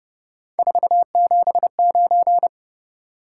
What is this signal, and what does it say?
Morse code, heard as a keyed tone, about 38 wpm, 700 Hz. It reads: 479
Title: Speed-Racing Top 180 QSO Elements 30wpm